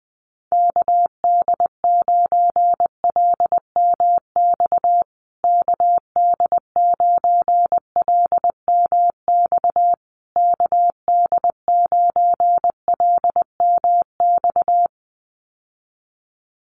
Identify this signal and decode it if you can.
Morse code, heard as a keyed tone, about 20 wpm, 700 Hz. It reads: KD9LMX KD9LMX KD9LMX
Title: USA Call Signs Repeated 3x 20wpm